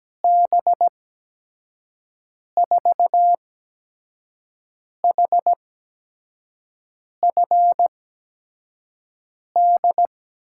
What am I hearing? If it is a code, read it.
Morse code, heard as a keyed tone, about 17 wpm, spaced as for 5 wpm, 700 Hz. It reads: B4HFD
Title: USA Call Signs - 6 Characters - 8x Character Spacing - 17wpm